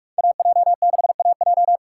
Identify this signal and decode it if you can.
Morse code, heard as a keyed tone, about 57 wpm, 700 Hz. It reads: AJ6AJ